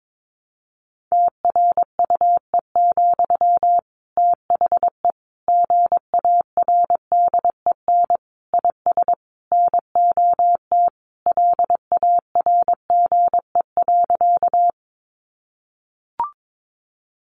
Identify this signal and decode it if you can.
Morse code, heard as a keyed tone, about 22 wpm, 700 Hz. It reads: TRUE, THE GARDEN IS NOT LARGE.
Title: Sentences from Top 700 Words 22wpm